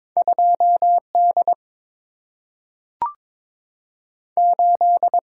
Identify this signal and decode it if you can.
Morse code, heard as a keyed tone, about 22 wpm, spaced as for 3 wpm, 700 Hz. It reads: 2 D8
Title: Single-Letter Single-Number 22wpm